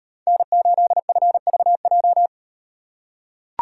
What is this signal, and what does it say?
Morse code, heard as a keyed tone, about 36 wpm, 700 Hz. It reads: N8FVJ